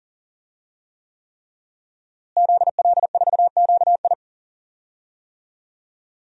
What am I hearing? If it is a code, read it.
Morse code, heard as a keyed tone, about 38 wpm, 700 Hz. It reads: ZL4QI
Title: DXCC-100 Call Signs - 5 Characters - Repeated 3x - 40wpm